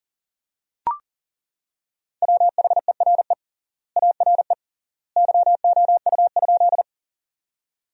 Morse code, heard as a keyed tone, about 40 wpm, 700 Hz. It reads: WHERE ARE YOU?
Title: 3x Word-Spacing - Sentences from Top 200 Words - 40wpm